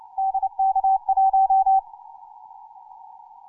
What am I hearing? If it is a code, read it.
Morse code, heard as a keyed tone, about 30 wpm, 770 Hz. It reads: DK1
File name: simple-file-list / hk0na 10mcw